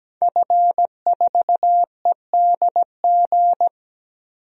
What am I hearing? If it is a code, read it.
Morse code, heard as a keyed tone, about 17 wpm, 700 Hz. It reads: F4EDG
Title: DXCC-100 Call Signs - 5 Characters - 17wpm